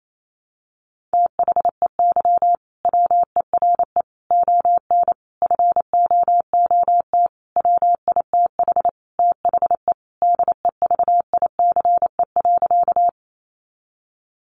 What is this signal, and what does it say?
Morse code, heard as a keyed tone, about 28 wpm, 700 Hz. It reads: THEY WERE ON FOOT WITH THE DEVICE.
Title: Sentences from Top 1500 Words 28wpm